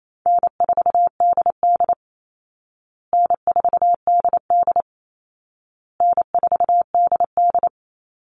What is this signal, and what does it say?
Morse code, heard as a keyed tone, about 28 wpm, 700 Hz. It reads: N4DD N4DD N4DD
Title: Call Signs Repeated 2x - 28wpm